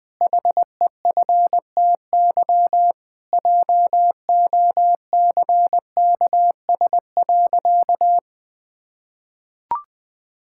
Morse code, heard as a keyed tone, about 20 wpm, 700 Hz. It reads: HEFTY JOCKS.
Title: Pangram Challenge v2 - 20wpm